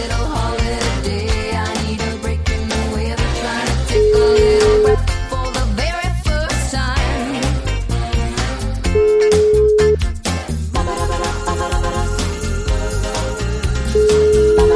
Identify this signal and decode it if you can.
Ringback tone (Europe, 425 Hz), with noise.